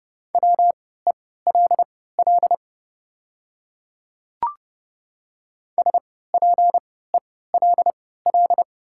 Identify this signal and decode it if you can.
Morse code, heard as a keyed tone, about 30 wpm, spaced as for 17 wpm, 690 Hz. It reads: WELL SPELL
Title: 3x Character Spacing - Top 200 Words - 30wpm